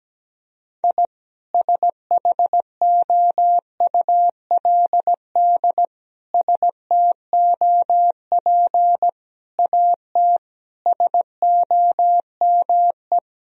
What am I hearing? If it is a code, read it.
Morse code, heard as a keyed tone, about 17 wpm, 700 Hz. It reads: I SHOULD STOP AT SOME
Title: Sentences from Top 1000 Words - Review Part 1 - 17wpm